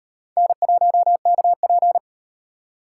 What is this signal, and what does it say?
Morse code, heard as a keyed tone, about 38 wpm, 700 Hz. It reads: N1KP